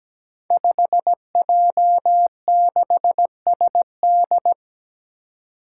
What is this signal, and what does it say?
Morse code, heard as a keyed tone, about 17 wpm, 700 Hz. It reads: 5J6SD
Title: Five Challenging Letter-Number Combinations 17wpm